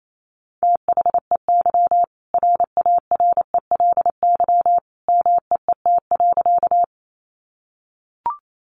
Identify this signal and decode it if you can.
Morse code, heard as a keyed tone, about 28 wpm, 700 Hz. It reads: THEY RARELY MEET.
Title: Sentences from Top 1400 Words 28wpm